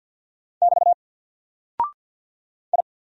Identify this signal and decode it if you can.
Morse code, heard as a keyed tone, about 48 wpm, 700 Hz. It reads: <BT> I
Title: Single Character (Letter:Number:Punctuation:Prosign) Alt 50wpm